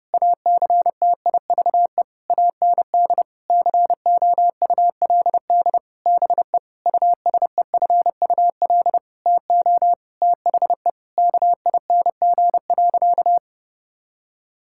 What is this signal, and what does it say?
Morse code, heard as a keyed tone, about 32 wpm, 700 Hz. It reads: ACTIVE AND COULD BE USEFUL TO THE KING.
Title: Sentences from Top 1000 Words - Review Part 6 - 30wpm